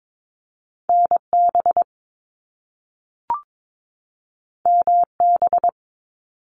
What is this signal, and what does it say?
Morse code, heard as a keyed tone, about 22 wpm, 700 Hz. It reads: NB MB
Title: Canadian Provinces and Territories 22wpm